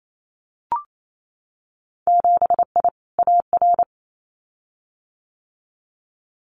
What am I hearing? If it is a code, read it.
Morse code, heard as a keyed tone, about 28 wpm, 700 Hz. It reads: 7I AR
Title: Winter Field Day Exchanges Encore 28wpm